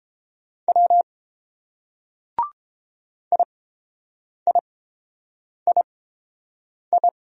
Morse code, heard as a keyed tone, about 30 wpm, spaced as for 7 wpm, 700 Hz. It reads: W IIII